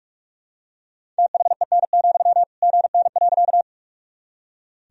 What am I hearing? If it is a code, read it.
Morse code, heard as a keyed tone, about 46 wpm, 700 Hz. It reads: THEN, GN.
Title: Sentences from Top 300 Words and Abbreviations 45wpm